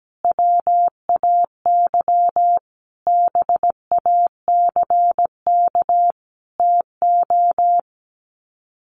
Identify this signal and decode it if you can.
Morse code, heard as a keyed tone, about 17 wpm, 700 Hz. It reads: WAY BACK TO
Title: Sets of 5 Words Encore 17wpm